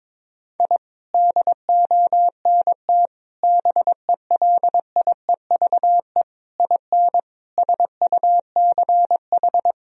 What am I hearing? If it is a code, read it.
Morse code, heard as a keyed tone, about 22 wpm, 700 Hz. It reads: I DONT BELIEVE IN SUCH
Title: Sentences from Top 1000 Words - Review Part 2 - 22wpm